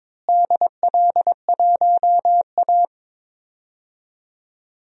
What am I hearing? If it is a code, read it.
Morse code, heard as a keyed tone, about 22 wpm, 700 Hz. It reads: DL1A
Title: DXCC-100 Call Signs - 4 Characters - Repeated 3x - 22wpm